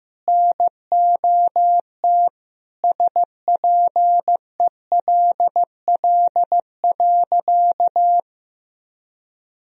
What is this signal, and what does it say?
Morse code, heard as a keyed tone, about 15 wpm, 700 Hz. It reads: NOT SPELL.
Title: Sentences from Top 1000 Words - Review Part 1 - 15wpm